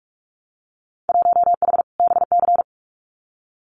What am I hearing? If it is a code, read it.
Morse code, heard as a keyed tone, about 46 wpm, 700 Hz. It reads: 1H BC
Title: Winter Field Day Exchanges 45wpm